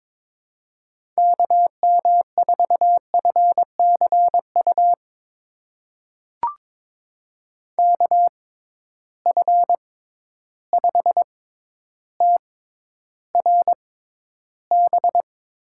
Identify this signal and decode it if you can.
Morse code, heard as a keyed tone, about 22 wpm, spaced as for 11 wpm, 700 Hz. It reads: KM4FCU KF5TRB
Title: USA Call Signs - 6 Characters - 6x Character Spacing - 22wpm